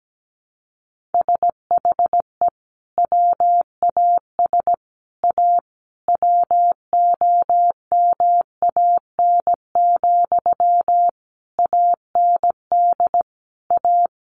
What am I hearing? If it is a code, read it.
Morse code, heard as a keyed tone, about 17 wpm, 700 Hz. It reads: SHE WAS A WOMAN, AND A
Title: Sentences from Top 1400 Words 17wpm